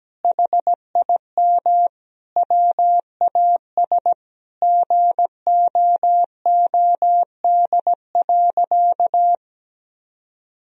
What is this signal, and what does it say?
Morse code, heard as a keyed tone, about 17 wpm, 700 Hz. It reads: HIM WAS GOOD.
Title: Sentences from Top 1000 Words - Review Part 4 - 17wpm